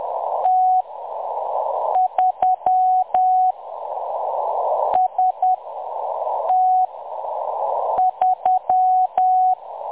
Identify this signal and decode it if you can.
Morse code, heard as a keyed tone, about 10 wpm, spaced as for 6 wpm, 740 Hz. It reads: T3 ST3
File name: M23-ST3- -0858utc-10184khz